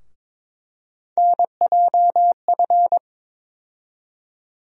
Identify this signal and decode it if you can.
Morse code, heard as a keyed tone, about 22 wpm, 700 Hz. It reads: NJF
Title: USA Call Sign Postfixes - 3 Letters 22wpm